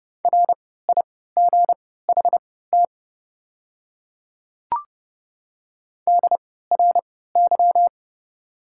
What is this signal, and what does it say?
Morse code, heard as a keyed tone, about 30 wpm, spaced as for 17 wpm, 690 Hz. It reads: RIGHT DRY
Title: 3x Character Spacing - Top 500 Words - 30wpm